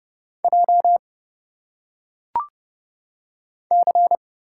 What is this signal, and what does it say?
Morse code, heard as a keyed tone, about 30 wpm, 700 Hz. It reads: J C
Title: Single Letters - Phonetic Pronunciation 30wpm